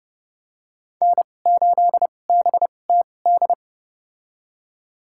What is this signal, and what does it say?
Morse code, heard as a keyed tone, about 30 wpm, spaced as for 22 wpm, 700 Hz. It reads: N8BTD
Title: USA Call Signs - 5 Characters - 2x Character Spacing - 30wpm